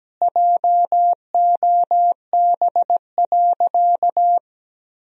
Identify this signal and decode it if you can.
Morse code, heard as a keyed tone, about 17 wpm, 700 Hz. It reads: JOB.
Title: Sentences from Top 1000 Words - Review Part 3 - 17wpm